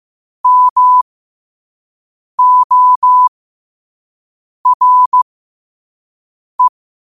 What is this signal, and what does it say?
Morse code, heard as a keyed tone, about 15 wpm, spaced as for 5 wpm, 1000 Hz. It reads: MORE
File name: qotdfiles / 5wpm